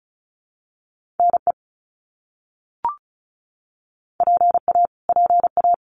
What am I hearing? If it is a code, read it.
Morse code, heard as a keyed tone, about 34 wpm, 700 Hz. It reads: NE PA PA
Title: US States Repeated 3x 35wpm